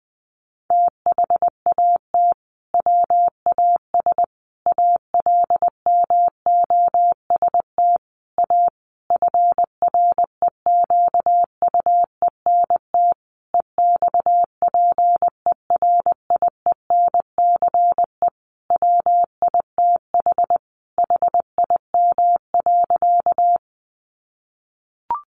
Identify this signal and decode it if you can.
Morse code, heard as a keyed tone, about 20 wpm, 700 Hz. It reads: THAT WAS ALMOST A FREQUENT EXPERIENCE WITH HIM.
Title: Sentences from Top 1400 Words 20wpm